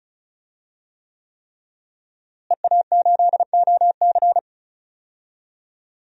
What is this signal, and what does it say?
Morse code, heard as a keyed tone, about 34 wpm, 700 Hz. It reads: EA8OC
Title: DXCC-100 Call Signs - 5 Characters - Repeated 3x - 35wpm